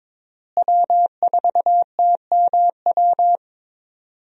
Morse code, heard as a keyed tone, about 22 wpm, 700 Hz. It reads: W4TMW
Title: USA Call Signs - 5 Characters - Repeated 2x - 22wpm